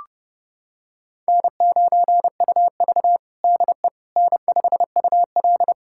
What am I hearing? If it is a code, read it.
Morse code, heard as a keyed tone, about 32 wpm, 700 Hz. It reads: N9UV DE N5UL
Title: USA Callsign DE Callsign - 4 Characters - 30wpm